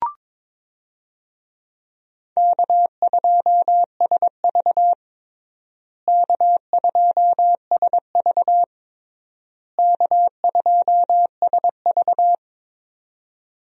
Morse code, heard as a keyed tone, about 22 wpm, 700 Hz. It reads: K2SV K2SV K2SV
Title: USA Call Signs - 4 Characters - Repeated 3x - 22wpm